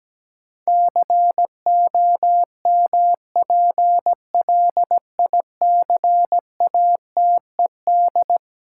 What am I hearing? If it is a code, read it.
Morse code, heard as a keyed tone, about 17 wpm, 700 Hz. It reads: COMPLICATED
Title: Sentences from Top 1200 Words - New Word Drill - 17wpm